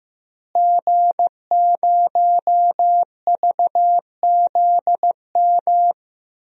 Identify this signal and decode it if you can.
Morse code, heard as a keyed tone, about 15 wpm, 700 Hz. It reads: G0VZM
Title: DXCC-100 Call Signs - 5 Characters - Repeated 3x - 15wpm